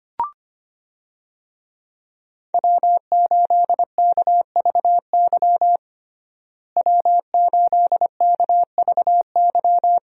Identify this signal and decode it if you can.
Morse code, heard as a keyed tone, about 25 wpm, 700 Hz. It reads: W8KVY W8KVY
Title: USA Call Signs - 5 Characters - Repeated 2x - 25wpm